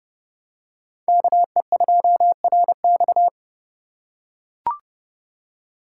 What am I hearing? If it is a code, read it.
Morse code, heard as a keyed tone, about 30 wpm, 700 Hz. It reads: KE2RX